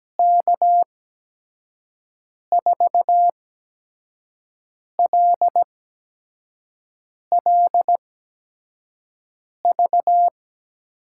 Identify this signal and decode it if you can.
Morse code, heard as a keyed tone, about 17 wpm, spaced as for 5 wpm, 700 Hz. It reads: K4LLV
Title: USA Call Signs - 6 Characters - 8x Character Spacing - 17wpm